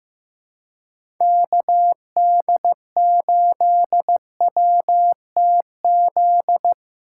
Morse code, heard as a keyed tone, about 15 wpm, 700 Hz. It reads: KD8WTZ
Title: USA Call Signs - 2x Character Spacing - 15wpm